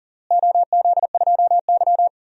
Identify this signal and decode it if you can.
Morse code, heard as a keyed tone, about 40 wpm, 690 Hz. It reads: OZ2Y
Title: DXCC-100 Call Signs - 4 Characters - Repeated 2x - 40wpm